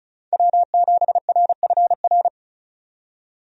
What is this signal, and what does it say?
Morse code, heard as a keyed tone, about 36 wpm, 700 Hz. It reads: W7RFR
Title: USA Call Signs - 5 Characters - 35wpm